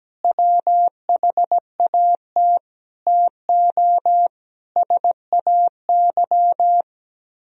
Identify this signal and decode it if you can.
Morse code, heard as a keyed tone, about 18 wpm, 700 Hz. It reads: WHAT TO SAY
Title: Speed-Racing - Sets of 5 Words 17wpm